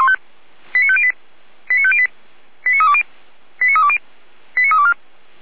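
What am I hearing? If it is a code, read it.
Five-tone selective call (ZVEI), heard as quick runs of five tones, tones of about 70 ms. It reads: 89508 89508 89320 89320 89324
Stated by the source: ZVEI five-tone call received by SDR, from Feuerwehr Orlamünde recording server.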